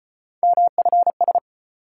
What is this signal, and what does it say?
Morse code, heard as a keyed tone, about 35 wpm, 700 Hz. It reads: MFS